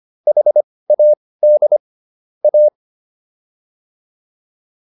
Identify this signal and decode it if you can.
Morse code, heard as a keyed tone, about 25 wpm, spaced as for 18 wpm, 600 Hz. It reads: HAD A